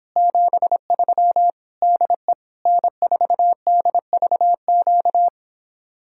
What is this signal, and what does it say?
Morse code, heard as a keyed tone, about 29 wpm, 700 Hz. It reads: 73 DE N4DVQ